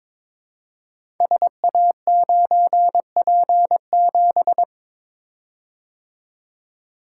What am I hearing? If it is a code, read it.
Morse code, heard as a keyed tone, about 21 wpm, 700 Hz. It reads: SA9P7
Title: Code Groups - 5 Letter:Numbers - 22wpm